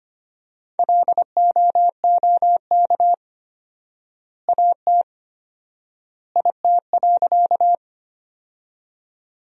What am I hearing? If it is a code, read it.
Morse code, heard as a keyed tone, about 25 wpm, 700 Hz. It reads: LOOK AT IT.
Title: 4x Word-Spacing - Sentences from Top 100 Words - 25wpm